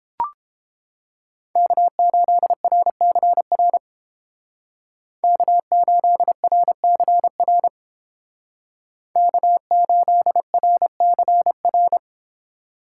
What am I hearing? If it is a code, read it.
Morse code, heard as a keyed tone, about 30 wpm, 700 Hz. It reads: K8RCR K8RCR K8RCR